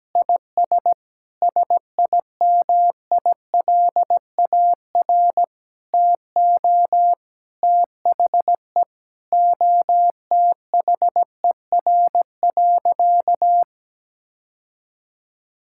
Morse code, heard as a keyed tone, about 17 wpm, 700 Hz. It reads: IS SIMILAR TO THE OTHER.
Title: Sentences from Top 1000 Words - Review Part 5 - 17wpm